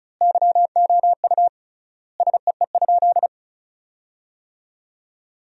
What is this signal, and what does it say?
Morse code, heard as a keyed tone, about 36 wpm, 700 Hz. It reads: YOU SEE?
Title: 3x Word-Spacing - Sentences from Top 100 Words - 35wpm